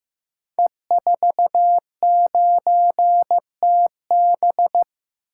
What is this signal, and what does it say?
Morse code, heard as a keyed tone, about 15 wpm, 700 Hz. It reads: E49TB
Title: Code Groups - 5 Letter:Numbers - 15wpm